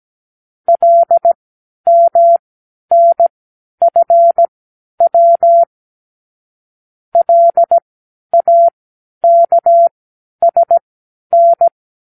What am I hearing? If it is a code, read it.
Morse code, heard as a keyed tone, about 17 wpm, spaced as for 10 wpm, 680 Hz. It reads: LMNFW LAKSN
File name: k7qo / Lesson 36 (A-X)